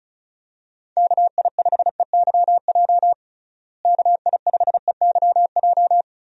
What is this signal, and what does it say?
Morse code, heard as a keyed tone, about 34 wpm, 700 Hz. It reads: KI5EYJ KI5EYJ
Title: USA Call Signs - 6 Characters - Repeated 2x - 35wpm